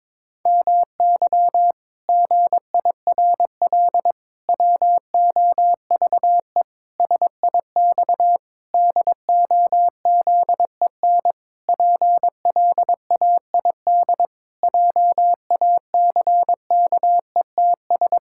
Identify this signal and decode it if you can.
Morse code, heard as a keyed tone, about 22 wpm, 700 Hz. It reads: MY GIRL WOVE SIX DOZEN PLAID JACKETS
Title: Pangram Challenge v2 - 22wpm